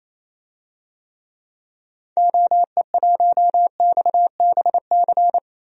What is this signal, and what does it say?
Morse code, heard as a keyed tone, about 28 wpm, 700 Hz. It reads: OE1XBC
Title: DXCC-100 Call Signs - 6 Characters - 28wpm